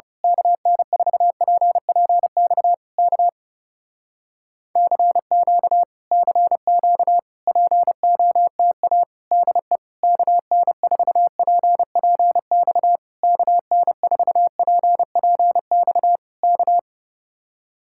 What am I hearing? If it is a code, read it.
Morse code, heard as a keyed tone, about 28 wpm, 700 Hz. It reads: KN4PPX K CQ CQ POTA DE KN4PPX KN4PPX K
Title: Speed-Racing Common QSO Phrases Encore - 30wpm